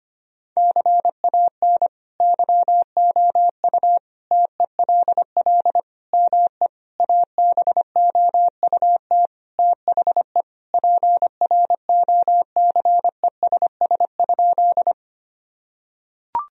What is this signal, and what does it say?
Morse code, heard as a keyed tone, about 25 wpm, 700 Hz. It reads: CAN YOU TELL ME ABOUT THE PROCESS?
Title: Sentences from Top 1000 Words - Review Part 3 - 25wpm